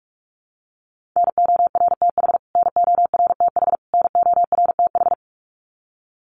Morse code, heard as a keyed tone, about 44 wpm, 700 Hz. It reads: NORTH NORTH NORTH
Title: Top 500 Words Repeated 45wpm